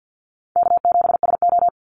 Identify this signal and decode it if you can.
Morse code, heard as a keyed tone, about 50 wpm, 700 Hz. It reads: K7SO
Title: NA SOTA Activator Callsigns 50wpm